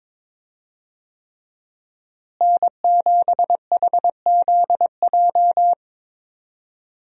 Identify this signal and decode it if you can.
Morse code, heard as a keyed tone, about 22 wpm, 700 Hz. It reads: N7HZJ
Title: USA Call Signs - 5 Characters - Repeated 3x - 22wpm